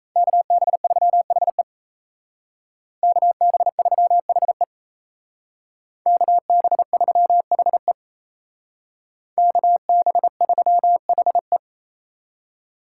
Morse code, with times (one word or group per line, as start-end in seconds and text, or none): 0.2-1.6 KB3HE
3.0-4.6 KB3HE
6.1-7.9 KB3HE
9.4-11.6 KB3HE